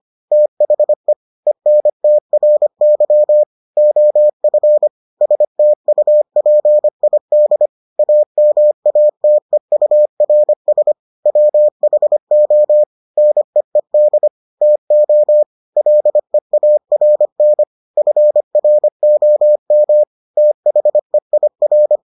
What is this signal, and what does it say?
Morse code, heard as a keyed tone, about 25 wpm, 600 Hz. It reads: THE ENTRY OF STUPID AMATEURS WHO NEED TO LEARN FROM THEIR